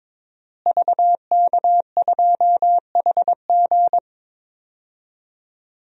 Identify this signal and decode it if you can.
Morse code, heard as a keyed tone, about 22 wpm, 700 Hz. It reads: VK2HG
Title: DXCC-100 Call Signs - 5 Characters - Repeated 3x - 22wpm